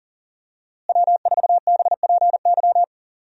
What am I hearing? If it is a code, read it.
Morse code, heard as a keyed tone, about 40 wpm, 700 Hz. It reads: W4BPY
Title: USA Call Signs - 5 Characters - Repeated 3x - 40wpm